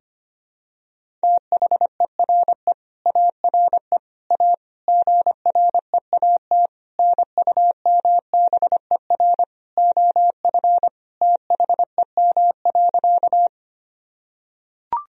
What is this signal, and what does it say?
Morse code, heard as a keyed tone, about 25 wpm, 700 Hz. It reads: THERE ARE A GREAT NUMBER OF THEM.
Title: Sentences from Top 1000 Words - Review Part 1 - 25wpm